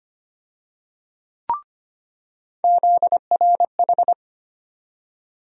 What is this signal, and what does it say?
Morse code, heard as a keyed tone, about 25 wpm, 700 Hz. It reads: ZRH
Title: USA Call Sign Postfixes - 3 Letters 25wpm